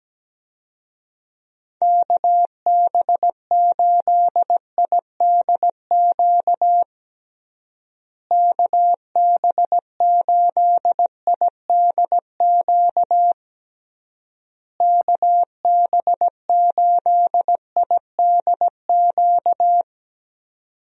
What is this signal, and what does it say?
Morse code, heard as a keyed tone, about 17 wpm, 700 Hz. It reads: KB8IDQ KB8IDQ KB8IDQ
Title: USA Call Signs - 6 Characters - Repeated 3x - 17wpm